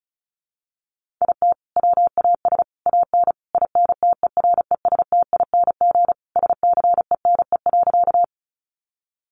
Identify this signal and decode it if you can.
Morse code, heard as a keyed tone, about 34 wpm, 700 Hz. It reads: IT WAS AN INTERESTING SCENE.
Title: Sentences from Top 1100 Words 35wpm